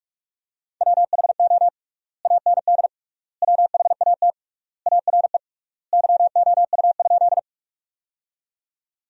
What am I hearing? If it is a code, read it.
Morse code, heard as a keyed tone, about 46 wpm, 700 Hz. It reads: WHO AND WHAT ARE YOU?
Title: 3x Word-Spacing - Sentences from Top 100 Words - 45wpm